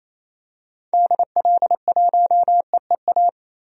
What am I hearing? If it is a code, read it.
Morse code, heard as a keyed tone, about 28 wpm, 700 Hz. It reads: DL1EEA
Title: DXCC-100 Call Signs - 6 Characters - 28wpm